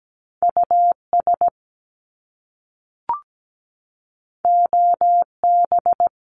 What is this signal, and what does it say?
Morse code, heard as a keyed tone, about 17 wpm, 700 Hz. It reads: US OB
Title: Top Two Letter Combinations in Words 17wpm